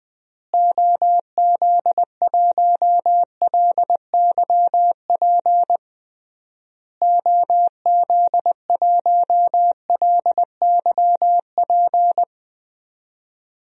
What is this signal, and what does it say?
Morse code, heard as a keyed tone, about 20 wpm, 700 Hz. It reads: OZ1LYP OZ1LYP
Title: DXCC-100 Call Signs - 6 Characters - Repeated 3x - 20wpm